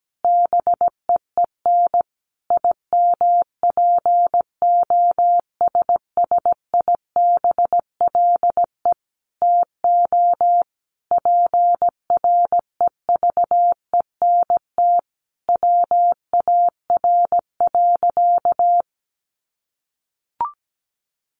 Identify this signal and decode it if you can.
Morse code, heard as a keyed tone, about 17 wpm, 700 Hz. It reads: BEEN IMPOSSIBLE TO PREVENT WAR.
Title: Sentences from Top 1100 Words 17wpm